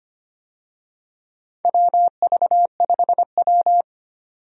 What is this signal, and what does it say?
Morse code, heard as a keyed tone, about 25 wpm, 700 Hz. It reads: WV5W
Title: USA Call Signs - 4 Characters - 25wpm